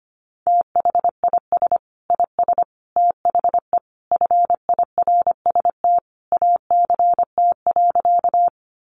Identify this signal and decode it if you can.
Morse code, heard as a keyed tone, about 25 wpm, 700 Hz. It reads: THIS IS THE FIRST ACT.
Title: Sentences from Top 200 Words 25wpm